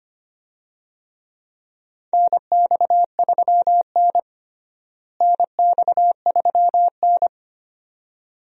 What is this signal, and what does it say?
Morse code, heard as a keyed tone, about 25 wpm, 700 Hz. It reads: NX3N NX3N
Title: USA Call Signs - 4 Characters - Repeated 2x - 25wpm